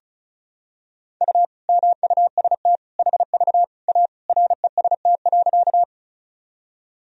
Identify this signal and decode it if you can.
Morse code, heard as a keyed tone, about 36 wpm, 700 Hz. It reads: U MUST HV A REST.
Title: Sentences from Top 500 Words and Abbreviations 35wpm